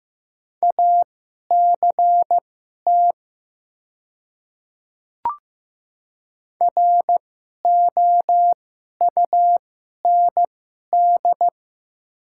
Morse code, heard as a keyed tone, about 15 wpm, spaced as for 11 wpm, 700 Hz. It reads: ACT ROUND
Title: 2x Character Spacing - Top 300 Words - 15wpm